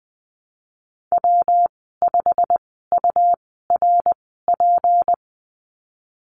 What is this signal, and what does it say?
Morse code, heard as a keyed tone, about 20 wpm, spaced as for 14 wpm, 700 Hz. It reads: W5URP
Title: 161 Lesson 26g - Letter g - Call Signs 20wpm